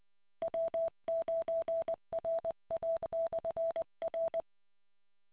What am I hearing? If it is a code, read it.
Morse code, heard as a keyed tone, about 20 wpm, 660 Hz. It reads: W9RA/R